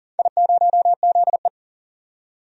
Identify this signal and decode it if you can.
Morse code, heard as a keyed tone, about 37 wpm, 700 Hz. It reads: I0ZE
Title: DXCC-100 Call Signs - 4 Characters - 40wpm